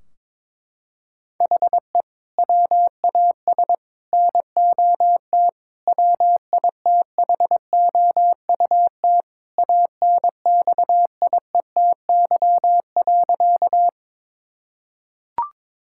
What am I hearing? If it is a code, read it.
Morse code, heard as a keyed tone, about 22 wpm, 700 Hz. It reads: HE WAS NOT WITHOUT ANXIETY.
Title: Sentences from Top 1600 Words 22wpm